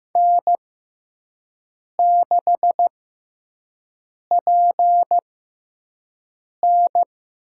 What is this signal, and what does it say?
Morse code, heard as a keyed tone, about 15 wpm, spaced as for 5 wpm, 700 Hz. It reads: N6PN
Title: USA Call Signs - 4 Characters - 6x Character Spacing - 15wpm